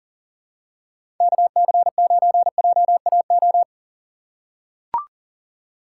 Morse code, heard as a keyed tone, about 40 wpm, 700 Hz. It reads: KC9JAO
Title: USA Call Signs - 6 Characters - 6x Character Spacing - 40wpm